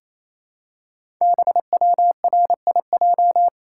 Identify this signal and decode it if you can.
Morse code, heard as a keyed tone, about 28 wpm, 700 Hz. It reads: BWRIJ
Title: Code Groups - 5 Letters - 28wpm